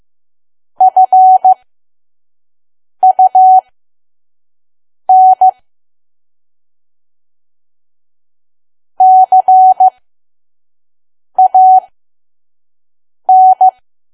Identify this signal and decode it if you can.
Morse code, heard as a keyed tone, about 15 wpm, spaced as for 5 wpm, 750 Hz. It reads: FUN CAN
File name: Archive / 5 WPM files / PM